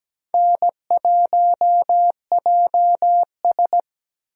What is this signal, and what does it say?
Morse code, heard as a keyed tone, about 17 wpm, 700 Hz. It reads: N1JS
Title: USA Call Signs - 3x Character Spacing - 17wpm